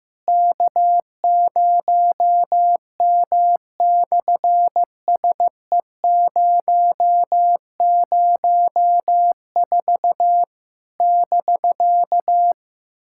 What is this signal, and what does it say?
Morse code, heard as a keyed tone, about 15 wpm, 700 Hz. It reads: K0M/SE004 <BK>
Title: SOTA QSO Practice 15wpm